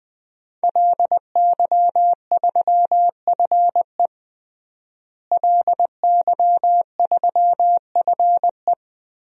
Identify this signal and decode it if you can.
Morse code, heard as a keyed tone, about 20 wpm, 700 Hz. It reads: LY3FE LY3FE
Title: DXCC-100 Call Signs - 5 Characters - Repeated 3x - 20wpm